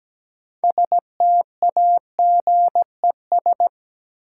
Morse code, heard as a keyed tone, about 17 wpm, 700 Hz. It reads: STAGES
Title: Sets of 5 Words Encore - New Word Drill - 17wpm